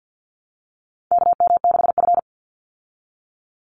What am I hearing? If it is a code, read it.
Morse code, heard as a keyed tone, about 50 wpm, 700 Hz. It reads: KM6F
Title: USA Call Signs 50wpm